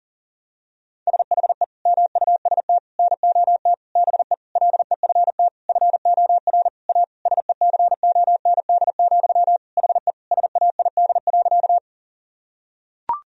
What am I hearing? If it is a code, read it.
Morse code, heard as a keyed tone, about 37 wpm, 700 Hz. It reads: SHE MUST NOT BE LEFT FOR A SECOND, HE SAID.
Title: Sentences from Top 1000 Words - Review Part 1 - 40wpm